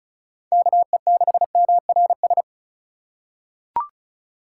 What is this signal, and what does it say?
Morse code, heard as a keyed tone, about 34 wpm, 700 Hz. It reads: KE6MRS
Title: USA Call Signs - 6 Characters - 8x Character Spacing - 35wpm